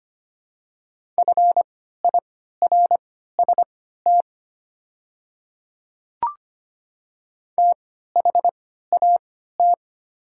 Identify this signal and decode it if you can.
Morse code, heard as a keyed tone, about 25 wpm, spaced as for 14 wpm, 700 Hz. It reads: FIRST THAT
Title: 3x Character Spacing - Top 200 Words - 25wpm